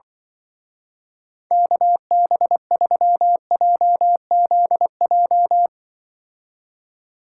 Morse code, heard as a keyed tone, about 23 wpm, 700 Hz. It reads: KB3JZJ
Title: Speed Racing - USA 2x3 Call Signs 20wpm